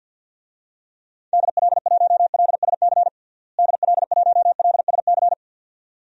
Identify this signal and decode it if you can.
Morse code, heard as a keyed tone, about 52 wpm, 700 Hz. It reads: DL1LSC DL1LSC
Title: DXCC-100 Call Signs - 6 Characters - Repeated 2x - 50wpm